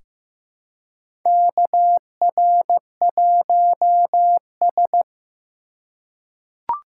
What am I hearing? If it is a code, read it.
Morse code, heard as a keyed tone, about 15 wpm, 700 Hz. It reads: KR1S
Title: USA Call Signs - 4x Character Spacing - 15wpm